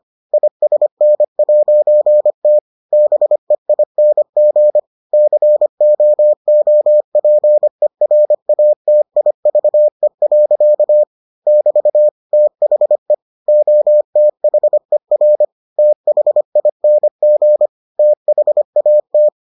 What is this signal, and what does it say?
Morse code, heard as a keyed tone, about 25 wpm, 600 Hz. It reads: ISN'T BEING COOPERATIVE. <BT> THE OTHER THING THAT